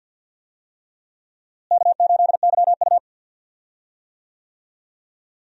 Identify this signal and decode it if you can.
Morse code, heard as a keyed tone, about 47 wpm, 700 Hz. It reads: K8YU
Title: USA Call Signs - 4 Characters - Repeated 3x - 50wpm